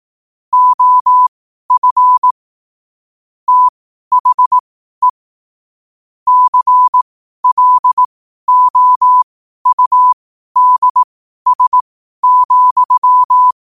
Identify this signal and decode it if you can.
Morse code, heard as a keyed tone, about 18 wpm, spaced as for 12 wpm, 1000 Hz. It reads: OF THE CLOUDS,
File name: qotdfiles / 12wpm